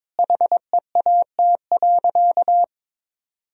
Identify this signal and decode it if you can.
Morse code, heard as a keyed tone, about 22 wpm, 700 Hz. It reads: HEAT.
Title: Sentences from Top 1000 Words - Review Part 3 - 22wpm